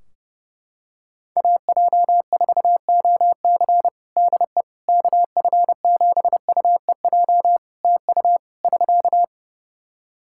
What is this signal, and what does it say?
Morse code, heard as a keyed tone, about 30 wpm, 700 Hz. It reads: AJ4OC DE KF7UEJ TU <SK>
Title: Practice QSO - Exchange 4 - Ending - 30wpm